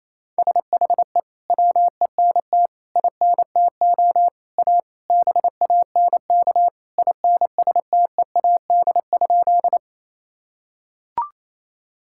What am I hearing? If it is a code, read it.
Morse code, heard as a keyed tone, about 27 wpm, 700 Hz. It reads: SHE WENT INTO A BANK INSTEAD?
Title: Sentences from Top 1000 Words - Review Part 6 - 28wpm